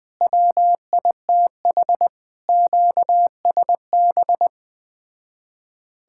Morse code, heard as a keyed tone, about 20 wpm, 700 Hz. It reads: WITH QSB